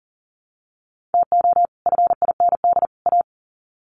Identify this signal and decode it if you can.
Morse code, heard as a keyed tone, about 40 wpm, 700 Hz. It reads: TO FIND A
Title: Sets of 3 Words 40wpm